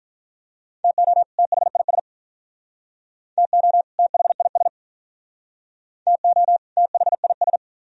Morse code, heard as a keyed tone, about 45 wpm, 700 Hz. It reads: TO THIS TO THIS TO THIS